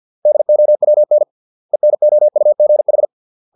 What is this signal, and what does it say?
Morse code, heard as a keyed tone, about 50 wpm, 600 Hz. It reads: DOWN ENOUGH
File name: 632 words / 632 Words 50WPM